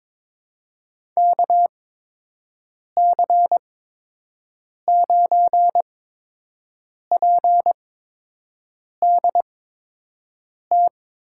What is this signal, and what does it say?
Morse code, heard as a keyed tone, about 23 wpm, spaced as for 6 wpm, 700 Hz. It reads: KC9PDT
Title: USA Call Signs - 6 Characters - 8x Character Spacing - 22wpm